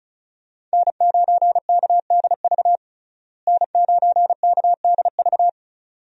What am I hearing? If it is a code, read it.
Morse code, heard as a keyed tone, about 36 wpm, 700 Hz. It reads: N9KDV N9KDV
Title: USA Call Signs - 5 Characters - Repeated 2x - 35wpm